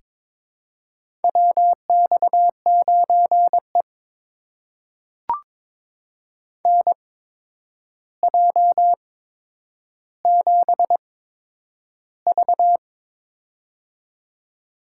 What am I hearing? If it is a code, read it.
Morse code, heard as a keyed tone, about 22 wpm, spaced as for 9 wpm, 700 Hz. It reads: WX9E NJ7V
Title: USA Call Signs - 4 Characters - 8x Character Spacing - 22wpm